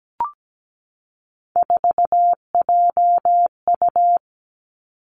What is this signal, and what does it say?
Morse code, heard as a keyed tone, about 17 wpm, 700 Hz. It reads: 4JU
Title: Three Challenging Letter-Number Combinations 17wpm